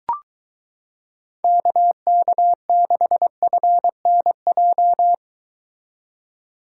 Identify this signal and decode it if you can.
Morse code, heard as a keyed tone, about 23 wpm, 700 Hz. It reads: KK6FNJ